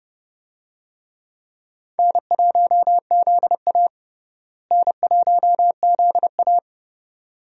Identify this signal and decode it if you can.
Morse code, heard as a keyed tone, about 30 wpm, 700 Hz. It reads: N1ZA N1ZA
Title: USA Call Signs - 4 Characters - Repeated 2x - 30wpm